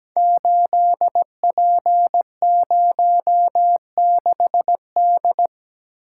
Morse code, heard as a keyed tone, about 17 wpm, 700 Hz. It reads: 8P06D
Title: Code Groups - 5 Letter:Numbers - 17wpm